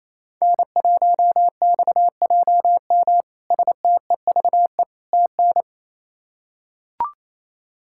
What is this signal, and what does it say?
Morse code, heard as a keyed tone, about 28 wpm, 700 Hz. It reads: N1XJM STEVE TN
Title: K1USN SST QSO Practice - 28wpm